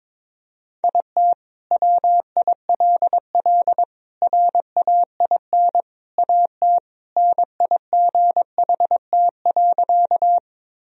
Morse code, heard as a keyed tone, about 22 wpm, 700 Hz. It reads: IT WILL RAIN AT NIGHT.
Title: Sentences from Top 1000 Words - Review Part 1 - 22wpm